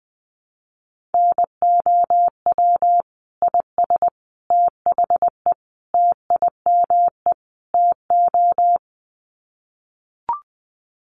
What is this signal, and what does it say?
Morse code, heard as a keyed tone, about 20 wpm, 700 Hz. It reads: NOW IS THE TIME TO
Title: Sets of 5 Words Encore2 20wpm